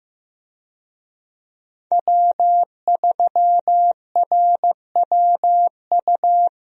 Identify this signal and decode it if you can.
Morse code, heard as a keyed tone, about 15 wpm, 700 Hz. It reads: W3RWU
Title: USA Call Signs - 5 Characters - Repeated 3x - 15wpm